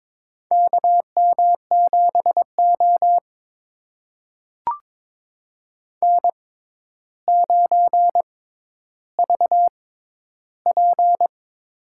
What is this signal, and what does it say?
Morse code, heard as a keyed tone, about 22 wpm, spaced as for 11 wpm, 700 Hz. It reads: KM7O N9VP
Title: USA Call Signs - 6x Character Spacing - 22wpm